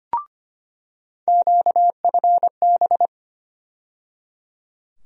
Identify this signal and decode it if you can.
Morse code, heard as a keyed tone, about 25 wpm, 700 Hz. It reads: QFB
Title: USA Call Sign Postfixes - 3 Letters 25wpm